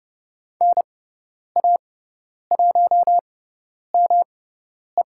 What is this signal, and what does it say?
Morse code, heard as a keyed tone, about 30 wpm, spaced as for 10 wpm, 700 Hz. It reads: NA1ME
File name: Lesson 18g - Letter m - Call Signs 30wpm